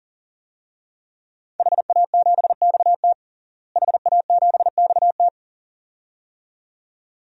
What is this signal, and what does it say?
Morse code, heard as a keyed tone, about 40 wpm, 700 Hz. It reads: HA7XT HA7XT
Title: DXCC-100 Call Signs - 5 Characters - Repeated 2x - 40wpm